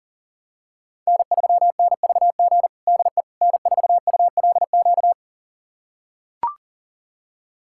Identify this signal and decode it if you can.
Morse code, heard as a keyed tone, about 40 wpm, 700 Hz. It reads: N3NVG DE N4ULQ
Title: USA Callsign DE Callsign - 5 Characters - 40wpm